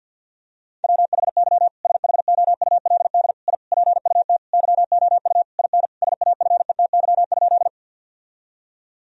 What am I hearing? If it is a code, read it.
Morse code, heard as a keyed tone, about 50 wpm, 700 Hz. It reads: WHY SHOULD I PUT YOU IN SAFETY?
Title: Sentences from Top 1000 Words - Review Part 4 - 50wpm